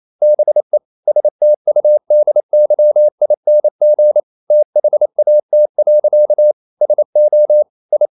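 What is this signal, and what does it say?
Morse code, heard as a keyed tone, about 28 wpm, 600 Hz. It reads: BE STUDYING THAT. SO I
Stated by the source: Morse code practice at 28 wpm, text of Ken Follett's novel Never in English, from CW med Gnister.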